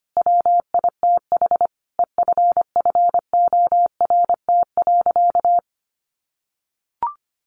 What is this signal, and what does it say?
Morse code, heard as a keyed tone, about 25 wpm, 700 Hz. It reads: WITH EFFORT.
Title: Sentences from Top 1200 Words 25wpm